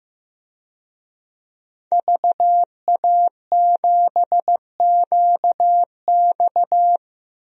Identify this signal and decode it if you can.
Morse code, heard as a keyed tone, about 15 wpm, 700 Hz. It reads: VA7QX
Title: DXCC-100 Call Signs - 5 Characters - 15wpm